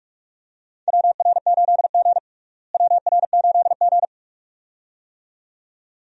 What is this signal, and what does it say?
Morse code, heard as a keyed tone, about 44 wpm, 700 Hz. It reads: WR8G WR8G
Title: USA Call Signs - 4 Characters - Repeated 2x - 45wpm